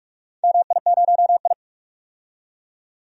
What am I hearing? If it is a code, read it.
Morse code, heard as a keyed tone, about 41 wpm, 700 Hz. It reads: MI0I
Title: DXCC-100 Call Signs - 4 Characters - 45wpm